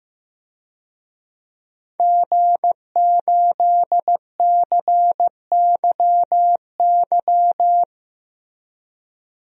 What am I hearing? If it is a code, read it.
Morse code, heard as a keyed tone, about 15 wpm, 700 Hz. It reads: G8CYY
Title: DXCC-100 Call Signs - 5 Characters - 15wpm